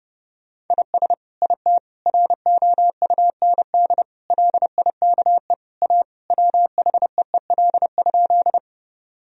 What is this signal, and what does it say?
Morse code, heard as a keyed tone, about 30 wpm, 700 Hz. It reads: IS IT ROUND LIKE A WHEEL?
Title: Sentences from Top 1000 Words - Review Part 3 - 30wpm